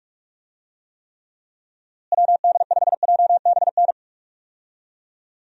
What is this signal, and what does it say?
Morse code, heard as a keyed tone, about 46 wpm, 700 Hz. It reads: WD5JBN
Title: USA Call Signs - 6 Characters - Repeated 2x - 45wpm